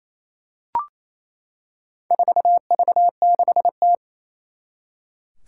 Morse code, heard as a keyed tone, about 28 wpm, 700 Hz. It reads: 4V6T
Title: Four Challenging Letter-Number Combinations 28wpm